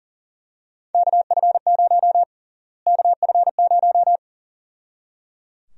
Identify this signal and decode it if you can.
Morse code, heard as a keyed tone, about 37 wpm, 700 Hz. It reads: KF0 KF0
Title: USA Call Sign Prefixes - Repeated 2x - 40wpm